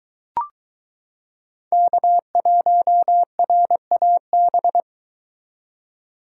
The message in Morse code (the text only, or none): K1RAB